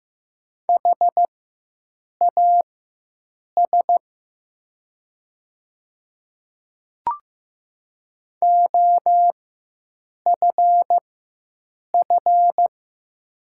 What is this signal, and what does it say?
Morse code, heard as a keyed tone, about 15 wpm, spaced as for 7 wpm, 700 Hz. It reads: HAS OFF
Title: 4x Character Spacing - Top 200 Words - 15wpm